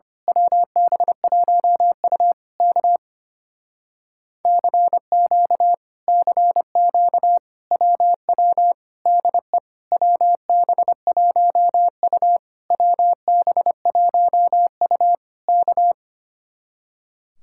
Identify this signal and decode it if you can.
Morse code, heard as a keyed tone, about 24 wpm, 700 Hz. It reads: WB1U K CQ CQ WW DE WB1U WB1U K
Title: Speed-Racing Common QSO Phrases - 25wpm